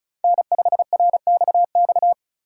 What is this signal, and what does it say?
Morse code, heard as a keyed tone, about 34 wpm, 700 Hz. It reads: N5RXX
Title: USA Call Signs - 5 Characters - 2x Character Spacing - 35wpm